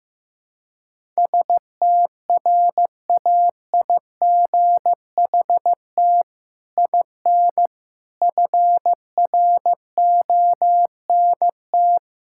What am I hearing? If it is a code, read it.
Morse code, heard as a keyed tone, about 15 wpm, 700 Hz. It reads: STRAIGHT IN FRONT
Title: Sentences from Top 1000 Words - Review Part 6 - 15wpm